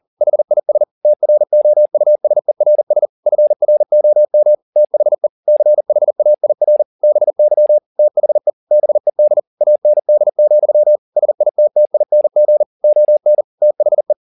Morse code, heard as a keyed tone, about 40 wpm, 600 Hz. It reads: HIS TROUSERS FROM THE CHAIR BY THE BED AND, SITTING ON THE